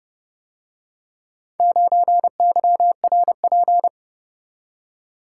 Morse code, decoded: 9YRP

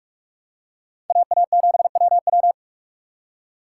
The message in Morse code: AA7WW

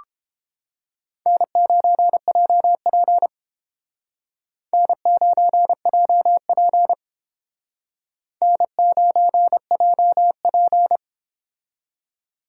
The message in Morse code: N9JP N9JP N9JP

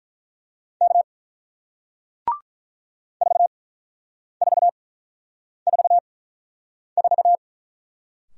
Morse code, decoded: K 4444

